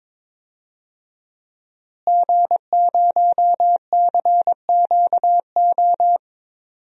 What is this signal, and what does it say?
Morse code, heard as a keyed tone, about 22 wpm, 700 Hz. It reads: G0CQO
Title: DXCC-100 Call Signs - 5 Characters - Repeated 2x - 22wpm